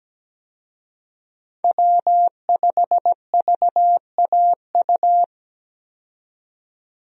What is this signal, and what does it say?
Morse code, heard as a keyed tone, about 17 wpm, 700 Hz. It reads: W5VAU
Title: USA Call Signs - 5 Characters - 17wpm